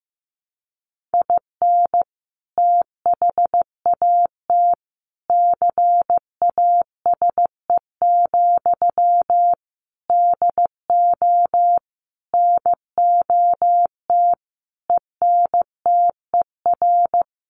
Morse code, heard as a keyed tone, about 14 wpm, 700 Hz. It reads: IN THAT CASE, DO NOT ENTER
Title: Sentences from Top 1300 Words 15wpm